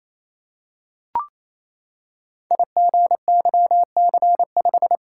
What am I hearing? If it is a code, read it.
Morse code, heard as a keyed tone, about 27 wpm, 700 Hz. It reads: IGYC5
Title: Code Groups - 5 Letter:Numbers - 28wpm